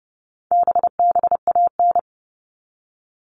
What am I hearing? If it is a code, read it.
Morse code, heard as a keyed tone, about 30 wpm, 700 Hz. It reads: BBAN
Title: All Cut Numbers - 4-Digit 30wpm